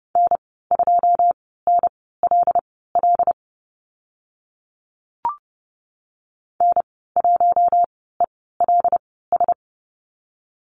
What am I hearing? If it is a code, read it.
Morse code, heard as a keyed tone, about 30 wpm, spaced as for 17 wpm, 700 Hz. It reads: N2NLL N1ELS
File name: Lesson 14g - Number 2 - Call Signs 30wpm